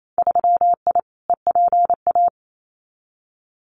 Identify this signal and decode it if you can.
Morse code, heard as a keyed tone, about 28 wpm, 700 Hz. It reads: 3I EPA
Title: Winter Field Day Exchanges 28wpm